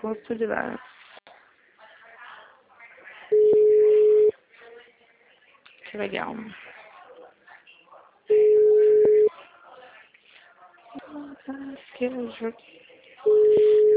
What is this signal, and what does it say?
Ringback tone (Europe, 425 Hz), with speech.